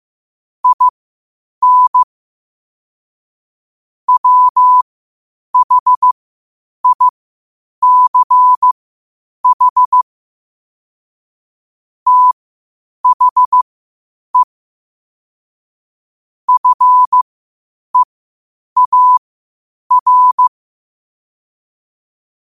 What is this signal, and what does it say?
Morse code, heard as a keyed tone, about 15 wpm, spaced as for 8 wpm, 1000 Hz. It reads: IN WHICH THE FEAR
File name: qotdfiles / 8wpm